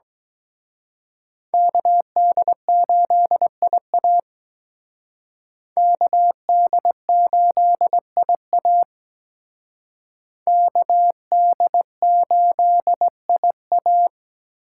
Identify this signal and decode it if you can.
Morse code, heard as a keyed tone, about 21 wpm, 700 Hz. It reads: KD8IA KD8IA KD8IA
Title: Speed Racing - USA Call Signs 17wpm